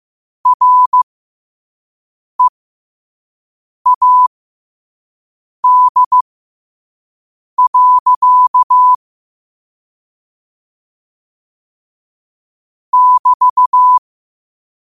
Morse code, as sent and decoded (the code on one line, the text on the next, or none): .-. .- -.. .-.-.- / -...-
READ. <BT>